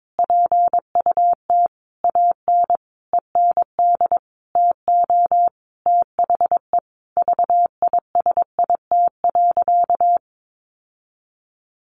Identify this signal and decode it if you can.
Morse code, heard as a keyed tone, about 22 wpm, 700 Hz. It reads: PUT AN END TO THE VISIT.
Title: Sentences from Top 1200 Words 22wpm